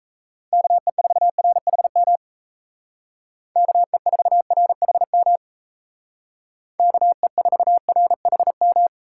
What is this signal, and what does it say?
Morse code, heard as a keyed tone, about 33 wpm, 700 Hz. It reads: KE4RHM KE4RHM KE4RHM
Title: Speed Racing - USA Call Signs 28wpm